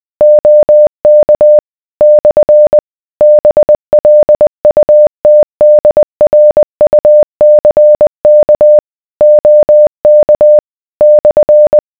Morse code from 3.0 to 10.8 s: BLUTDRUCK OK